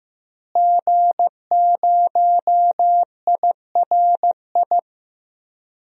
Morse code, heard as a keyed tone, about 16 wpm, 700 Hz. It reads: G0IRI
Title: DXCC-100 Call Signs - 5 Characters - Repeated 3x - 15wpm